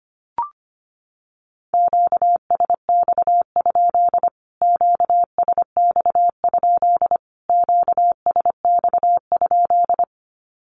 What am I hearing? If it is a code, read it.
Morse code, heard as a keyed tone, about 25 wpm, 700 Hz. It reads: QSX? QSX? QSX?
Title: Q-Codes Repeated 3x 25wpm